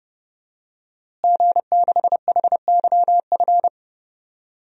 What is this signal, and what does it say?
Morse code, heard as a keyed tone, about 30 wpm, 700 Hz. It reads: G6HYF
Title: DXCC-100 Call Signs - 5 Characters - Repeated 3x - 30wpm